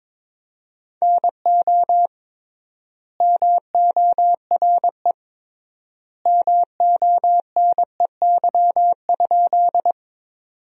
Morse code, heard as a keyed tone, about 22 wpm, 700 Hz. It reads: NO MORE MONEY?
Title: 3x Word-Spacing - Sentences from Top 400 Words - 22wpm